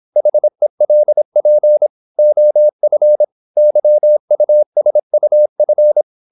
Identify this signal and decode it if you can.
Morse code, heard as a keyed tone, about 26 wpm, 600 Hz. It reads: HELP OF YUSUF
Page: Never 26wpm | CW med Gnister